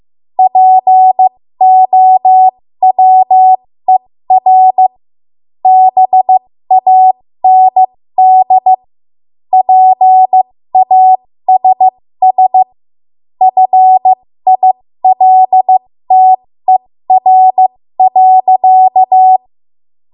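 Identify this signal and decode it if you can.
Morse code, heard as a keyed tone, about 15 wpm, spaced as for 13 wpm, 750 Hz. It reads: POWER BAND PASS FILTER.
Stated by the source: 13 wpm Morse code practice archive transmission.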